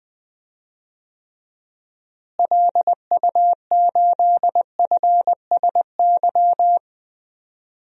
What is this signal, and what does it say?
Morse code, heard as a keyed tone, about 20 wpm, 700 Hz. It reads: LU8FSY
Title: DXCC-100 Call Signs - 6 Characters - Repeated 2x - 20wpm